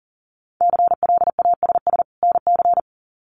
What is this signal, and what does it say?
Morse code, heard as a keyed tone, about 40 wpm, 700 Hz. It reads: CLASS NC
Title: Winter Field Day Exchanges 40wpm